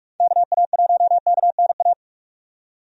Morse code, heard as a keyed tone, about 42 wpm, 700 Hz. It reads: KA1KNA
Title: USA Call Signs - 6 Characters - 45wpm